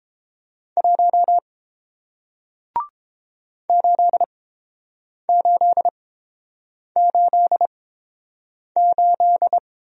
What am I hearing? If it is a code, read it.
Morse code, heard as a keyed tone, about 30 wpm, spaced as for 7 wpm, 700 Hz. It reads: 1 8888